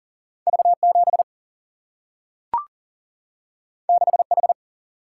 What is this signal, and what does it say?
Morse code, heard as a keyed tone, about 40 wpm, 700 Hz. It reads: V 7 6 H
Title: Single-Letter Single-Number 40wpm